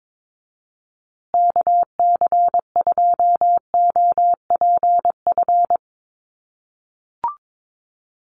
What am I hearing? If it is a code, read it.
Morse code, heard as a keyed tone, about 22 wpm, 700 Hz. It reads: KC2OPF
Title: USA Call Signs Repeated 2x 22wpm